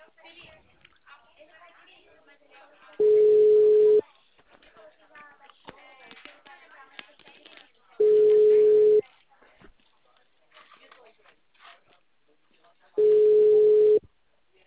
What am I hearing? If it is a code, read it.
Ringback tone (Europe, 425 Hz), with speech.